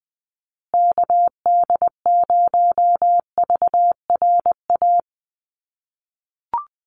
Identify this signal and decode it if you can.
Morse code, heard as a keyed tone, about 20 wpm, 700 Hz. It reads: KD0VRA
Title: USA Call Signs Repeated 2x 20wpm